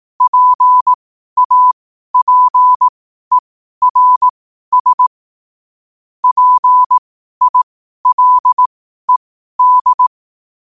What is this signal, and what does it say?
Morse code, heard as a keyed tone, about 18 wpm, spaced as for 12 wpm, 1000 Hz. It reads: PAPERS PILED
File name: qotdfiles / 12wpm